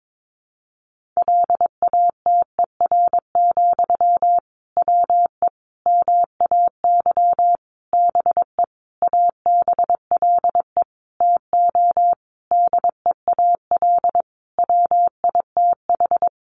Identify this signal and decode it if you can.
Morse code, heard as a keyed tone, about 22 wpm, 700 Hz. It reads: LATER, WE MAY BE ABLE TO DEAL WITH
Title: Sentences from Top 800 Words 22wpm